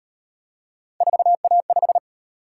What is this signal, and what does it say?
Morse code, heard as a keyed tone, about 36 wpm, 700 Hz. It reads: 4A5